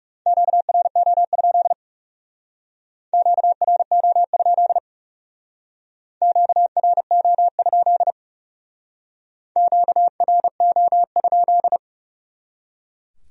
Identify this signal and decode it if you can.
Morse code, heard as a keyed tone, about 36 wpm, 700 Hz. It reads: QRO? QRO? QRO? QRO?
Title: Speed-Racing Q-Codes - 30wpm